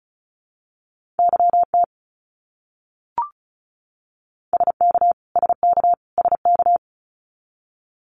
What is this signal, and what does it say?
Morse code, heard as a keyed tone, about 34 wpm, 700 Hz. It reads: YT SK SK SK
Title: Canadian Provinces and Territories - Repeated 3x - 35wpm